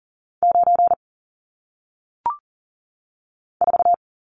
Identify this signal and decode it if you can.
Morse code, heard as a keyed tone, about 39 wpm, 700 Hz. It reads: 9 4
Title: Numbers - 1-Digit 40wpm